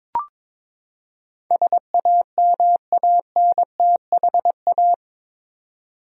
Morse code, heard as a keyed tone, about 22 wpm, 700 Hz. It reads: SAMANTHA
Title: USA Common Names - 22wpm